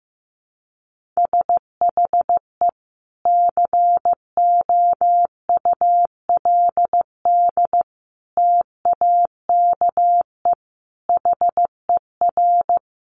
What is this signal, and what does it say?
Morse code, heard as a keyed tone, about 15 wpm, 700 Hz. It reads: SHE COULD TAKE HER
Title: Sentences from Top 900 Words 15wpm